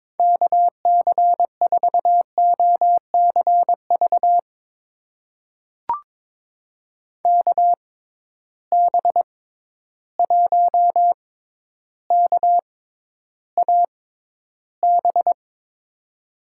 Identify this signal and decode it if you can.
Morse code, heard as a keyed tone, about 22 wpm, spaced as for 11 wpm, 700 Hz. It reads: KC4OCV KB1KAB
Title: USA Call Signs - 6 Characters - 6x Character Spacing - 22wpm